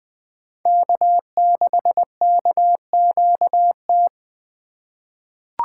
Morse code, heard as a keyed tone, about 21 wpm, 700 Hz. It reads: K6KQT
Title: USA Call Signs - 5 Characters - 2x Character Spacing - 20wpm